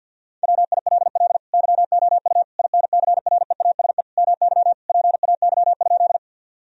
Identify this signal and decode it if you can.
Morse code, heard as a keyed tone, about 48 wpm, 700 Hz. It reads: WILL YOU INCREASE MY PAY?
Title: Sentences from Top 1000 Words - Review Part 4 - 50wpm